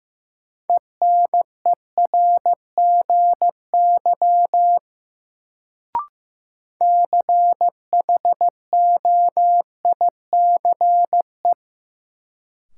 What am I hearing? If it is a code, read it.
Morse code, heard as a keyed tone, about 15 wpm, 700 Hz. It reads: ENERGY CHOICE
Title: Sentences from Top 900 Words - New Word Drill - 15wpm